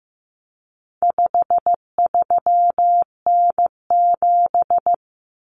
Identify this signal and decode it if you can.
Morse code, heard as a keyed tone, about 15 wpm, 700 Hz. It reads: 53N7
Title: Common Cut Numbers - 4-Digit 15wpm